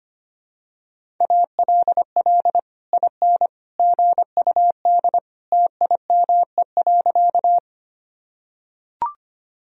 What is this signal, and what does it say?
Morse code, heard as a keyed tone, about 25 wpm, 700 Hz. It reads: ALL IN GUD TIME.
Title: Sentences from Top 200 Words and Abbreviations 25wpm